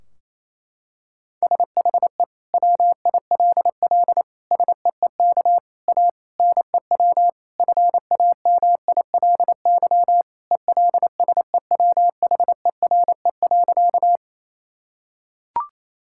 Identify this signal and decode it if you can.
Morse code, heard as a keyed tone, about 28 wpm, 700 Hz. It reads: SHE WILL SEEK A NEW FAMILY ELSEWHERE.
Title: Sentences from Top 1600 Words 28wpm